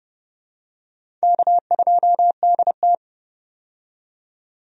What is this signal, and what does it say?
Morse code, heard as a keyed tone, about 32 wpm, 700 Hz. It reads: K2DT
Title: USA Call Signs - 4 Characters - 30wpm